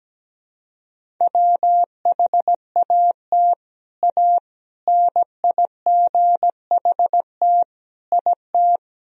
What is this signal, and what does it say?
Morse code, heard as a keyed tone, about 17 wpm, 700 Hz. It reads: WHAT A NIGHT IT
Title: Sentences from Top 300 Words and Abbreviations 17wpm